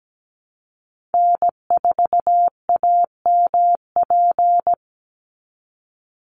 Morse code, heard as a keyed tone, about 17 wpm, 700 Hz. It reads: N4AMP
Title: USA Call Signs 17wpm